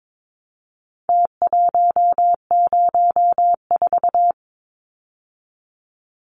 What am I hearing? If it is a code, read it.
Morse code, heard as a keyed tone, about 22 wpm, 700 Hz. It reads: T104
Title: Numbers - Contest Style - 4-Digit 22wpm